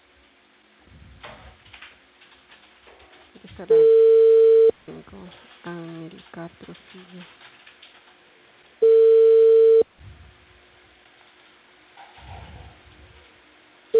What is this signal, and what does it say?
Ringback tone (China, 450 Hz), with speech.